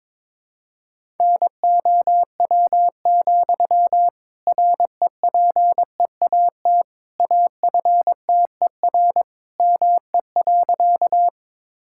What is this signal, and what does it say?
Morse code, heard as a keyed tone, about 22 wpm, 700 Hz. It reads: NOW, REPEAT AFTER ME.
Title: Sentences from Top 1600 Words 22wpm